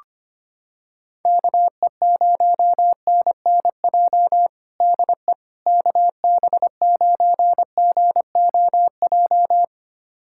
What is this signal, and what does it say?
Morse code, heard as a keyed tone, about 25 wpm, 700 Hz. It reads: KE0NNJ DE KB9GOJ
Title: USA Callsign DE Callsign - 6 Characters - 25wpm